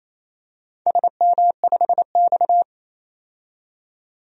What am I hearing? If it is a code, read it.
Morse code, heard as a keyed tone, about 28 wpm, 700 Hz. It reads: SM5X
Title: DXCC-100 Call Signs - 4 Characters - 28wpm